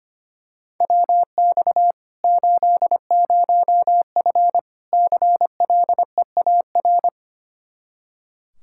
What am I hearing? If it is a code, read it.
Morse code, heard as a keyed tone, about 25 wpm, 700 Hz. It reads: WX 80F CLEAR
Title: Practice QSO - Exchange 2 - Info - 25wpm